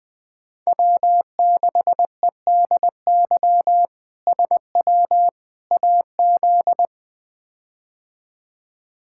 Morse code, heard as a keyed tone, about 20 wpm, 700 Hz. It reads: W6EDY SW AZ
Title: K1USN SST QSO Practice - 20wpm